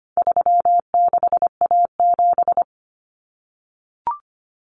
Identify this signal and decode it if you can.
Morse code, heard as a keyed tone, about 25 wpm, 700 Hz. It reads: 36A7
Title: Common Cut Numbers - 4-Digit 25wpm